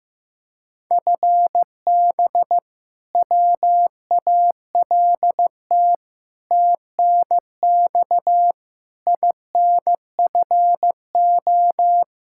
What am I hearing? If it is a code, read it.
Morse code, heard as a keyed tone, about 16 wpm, 700 Hz. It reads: FB WALT TNX INFO
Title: Practice QSO - Exchange 3 - More Info - 15wpm